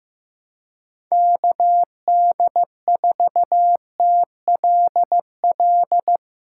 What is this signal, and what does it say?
Morse code, heard as a keyed tone, about 15 wpm, 700 Hz. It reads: KD4TLL
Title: USA Call Signs - 6 Characters - 4x Character Spacing - 15wpm